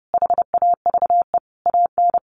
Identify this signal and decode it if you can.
Morse code, heard as a keyed tone, about 30 wpm, 700 Hz. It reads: HAVE AN
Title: Sentences from Top 1200 Words 30wpm